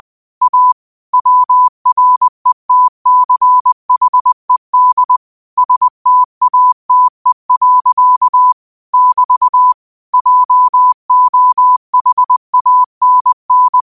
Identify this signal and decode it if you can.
Morse code, heard as a keyed tone, about 20 wpm, 1000 Hz. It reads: A WRETCHED STATE. <BT> JOHANN
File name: qotdfiles / 20wpm